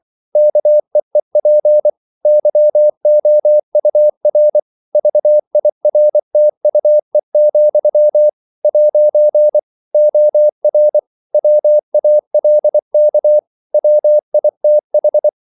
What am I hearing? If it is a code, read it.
Morse code, heard as a keyed tone, about 24 wpm, 600 Hz. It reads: KEEP YOUR VIRTUE, ' OR WALK WITH
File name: CW / Practice / If24